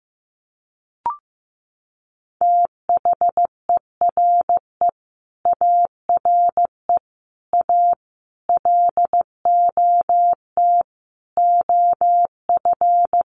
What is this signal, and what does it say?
Morse code, heard as a keyed tone, about 15 wpm, 700 Hz. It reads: THERE ARE A LOT OF
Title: Sets of 5 Words 15wpm